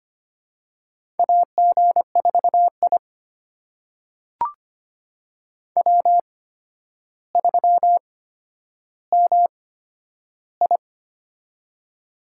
Morse code, heard as a keyed tone, about 25 wpm, spaced as for 11 wpm, 700 Hz. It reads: AG4I W3MI
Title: USA Call Signs - 4 Characters - 8x Character Spacing - 25wpm